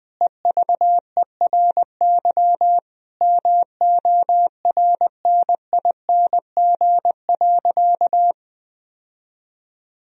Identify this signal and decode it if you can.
Morse code, heard as a keyed tone, about 20 wpm, 700 Hz. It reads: EVERY MORNING.
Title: Sentences from Top 1000 Words - Review Part 2 - 20wpm